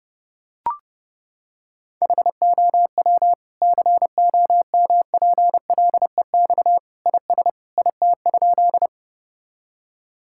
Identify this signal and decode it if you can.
Morse code, heard as a keyed tone, about 31 wpm, 700 Hz. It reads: HOW COMPLEX IS IT?
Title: Sentences from Top 1000 Words - Review Part 6 - 30wpm